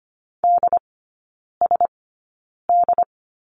Morse code, heard as a keyed tone, about 25 wpm, spaced as for 9 wpm, 700 Hz. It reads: DSD
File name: Lesson 17e - Letter u - Three Characters 25wpm